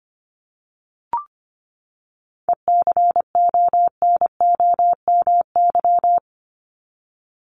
Morse code, heard as a keyed tone, about 25 wpm, 700 Hz. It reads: ECONOMY
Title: Top 501-750 Words 25wpm